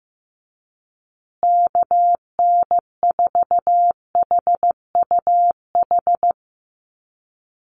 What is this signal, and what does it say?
Morse code, heard as a keyed tone, about 15 wpm, 700 Hz. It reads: KN4HUH
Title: USA Call Signs Repeated 3x 15wpm